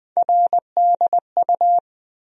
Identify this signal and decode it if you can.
Morse code, heard as a keyed tone, about 20 wpm, 700 Hz. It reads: RDU
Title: USA Call Sign Postfixes - 3 Letters 20wpm